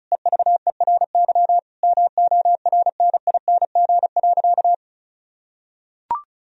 Morse code, heard as a keyed tone, about 36 wpm, 700 Hz. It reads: EVERY MORNING.
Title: Sentences from Top 1000 Words - Review Part 4 - 35wpm